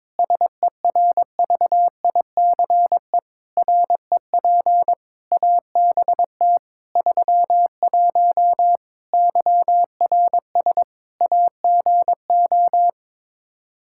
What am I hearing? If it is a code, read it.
Morse code, heard as a keyed tone, about 22 wpm, 700 Hz. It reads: SERVICE REP ABT 31 YRS AGO
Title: Practice QSO - Exchange 3 - More Info - 22wpm